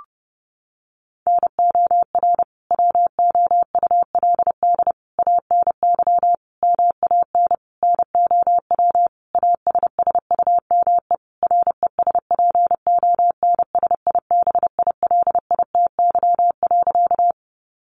Morse code, heard as a keyed tone, about 30 wpm, 700 Hz. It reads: NOR WOULD ANY MAN NOW ASSUME RESPONSIBILITY.
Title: Sentences from Top 1500 Words 30wpm